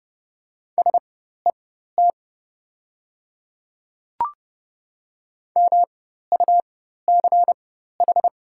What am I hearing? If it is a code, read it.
Morse code, heard as a keyed tone, about 30 wpm, spaced as for 14 wpm, 700 Hz. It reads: SET MUCH
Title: 4x Character Spacing - Top 200 Words - 30wpm